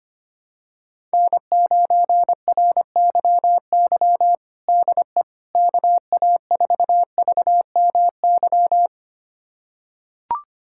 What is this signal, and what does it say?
Morse code, heard as a keyed tone, about 25 wpm, 700 Hz. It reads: N9RYY DE KA4VMY
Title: USA Callsign DE Callsign 25wpm